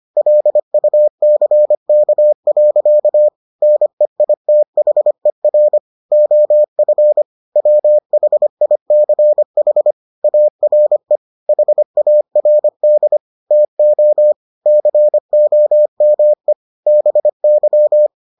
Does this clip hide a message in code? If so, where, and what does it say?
Yes: Morse code from 0.2 to 18.1 s, heard as a keyed tone, about 25 wpm, 600 Hz. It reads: LUCK. NEITHER OF WHICH ARE HARD TO COME BY